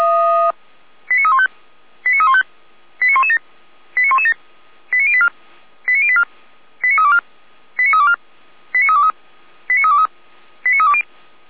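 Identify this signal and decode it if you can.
Five-tone selective call (ZVEI), heard as quick runs of five tones, tones of about 70 ms. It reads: 89315 89315 89117 89117 89084 89084 89324 89324 89323 89323 89320
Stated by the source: ZVEI five-tone selective call 89125 received on an SDR.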